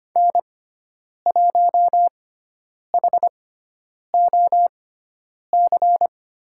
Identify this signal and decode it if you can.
Morse code, heard as a keyed tone, about 25 wpm, spaced as for 9 wpm, 700 Hz. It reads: N1HOC
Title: USA Call Signs - 5 Characters - 6x Character Spacing - 25wpm